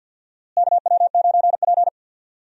Morse code, heard as a keyed tone, about 51 wpm, 700 Hz. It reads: KW9P
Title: USA Call Signs - 4 Characters - Repeated 3x - 50wpm